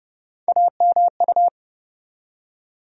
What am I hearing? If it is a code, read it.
Morse code, heard as a keyed tone, about 32 wpm, 700 Hz. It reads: AMU
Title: USA Call Sign Postfixes - 3 Letters 30wpm